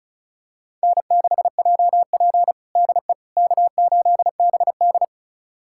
Morse code, heard as a keyed tone, about 36 wpm, 700 Hz. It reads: N6JP DE K8BD
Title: USA Callsign DE Callsign - 4 Characters - 35wpm